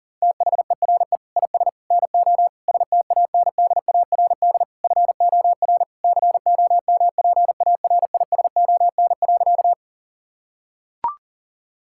Morse code, heard as a keyed tone, about 40 wpm, 700 Hz. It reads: THERE IS NO STANDARD FOR COMPARISON.
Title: Sentences from Top 1600 Words 40wpm